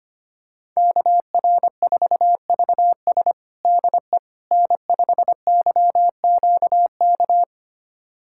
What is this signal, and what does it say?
Morse code, heard as a keyed tone, about 25 wpm, 700 Hz. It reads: KR4VS DE N5YQK
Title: USA Callsign DE Callsign - 5 Characters - 25wpm